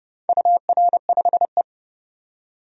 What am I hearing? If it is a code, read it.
Morse code, heard as a keyed tone, about 31 wpm, 700 Hz. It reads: UR5E
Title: DXCC-100 Call Signs - 4 Characters - 30wpm